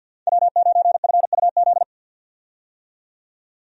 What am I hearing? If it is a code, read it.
Morse code, heard as a keyed tone, about 48 wpm, 700 Hz. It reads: W9FUZ
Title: USA Call Signs - 5 Characters - 50wpm